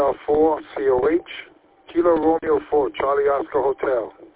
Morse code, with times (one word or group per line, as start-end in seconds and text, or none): none